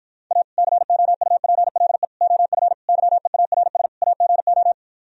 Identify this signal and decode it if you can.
Morse code, heard as a keyed tone, about 52 wpm, 700 Hz. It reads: A COUPLE OF YEARS AGO